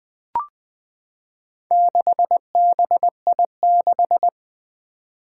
Morse code, heard as a keyed tone, about 20 wpm, 700 Hz. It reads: 6BI6
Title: Four Challenging Letter-Number Combinations 20wpm